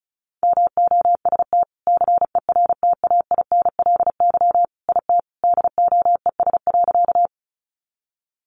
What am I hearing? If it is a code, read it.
Morse code, heard as a keyed tone, about 34 wpm, 700 Hz. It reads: MOST CERTAINLY IT DOES.
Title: Sentences from Top 900 Words 35wpm